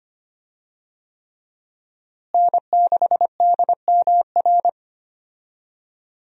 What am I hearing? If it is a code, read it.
Morse code, heard as a keyed tone, about 25 wpm, 700 Hz. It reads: N6DMR
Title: USA Call Signs - 5 Characters - Repeated 3x - 25wpm